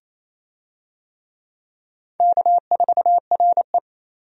Morse code, heard as a keyed tone, about 27 wpm, 700 Hz. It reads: K4RE
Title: USA Call Signs - 4 Characters - 28wpm